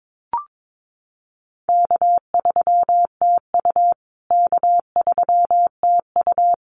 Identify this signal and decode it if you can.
Morse code, heard as a keyed tone, about 22 wpm, 700 Hz. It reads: K3TU K3TU
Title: USA Call Signs Repeated 3x 22wpm